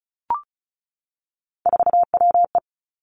Morse code, heard as a keyed tone, about 36 wpm, 700 Hz. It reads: 4WE
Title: Three Challenging Letter-Number Combinations 35wpm